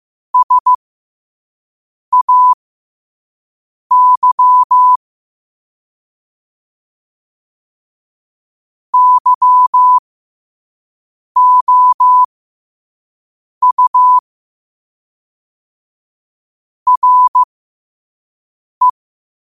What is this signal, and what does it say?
Morse code, heard as a keyed tone, about 15 wpm, spaced as for 5 wpm, 1000 Hz. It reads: SAY YOU RE
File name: qotdfiles / 5wpm